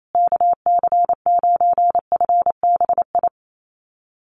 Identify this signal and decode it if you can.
Morse code, heard as a keyed tone, about 28 wpm, 700 Hz. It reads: KC9FBI
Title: USA Call Signs Repeated 2x 28wpm